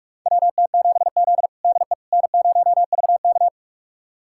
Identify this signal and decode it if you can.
Morse code, heard as a keyed tone, about 43 wpm, 700 Hz. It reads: WT7Z DE N0VK
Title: USA Callsign DE Callsign - 4 Characters - 45wpm